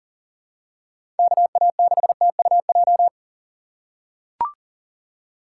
Morse code, heard as a keyed tone, about 40 wpm, 700 Hz. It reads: KA6TUJ
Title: USA Call Signs - 6x Character Spacing - 40wpm